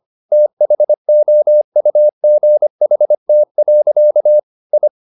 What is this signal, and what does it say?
Morse code, heard as a keyed tone, about 25 wpm, 600 Hz. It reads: THOUGHT. I
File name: the old man and the sea part 1 25wpm